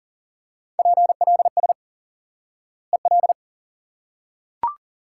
Morse code, heard as a keyed tone, about 40 wpm, 700 Hz. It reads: PLS EL
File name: IQE Course 3 - Lesson 89 - YAGI - Two Elements - 40wpm